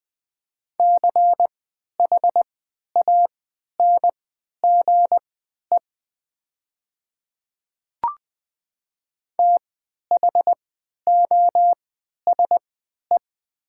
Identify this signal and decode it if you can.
Morse code, heard as a keyed tone, about 20 wpm, spaced as for 11 wpm, 700 Hz. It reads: CHANGE THOSE
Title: 3x Character Spacing - Top 300 Words - 20wpm